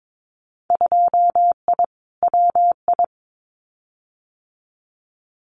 Morse code, heard as a keyed tone, about 22 wpm, 700 Hz. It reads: 2I WI
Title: Winter Field Day Exchanges 22wpm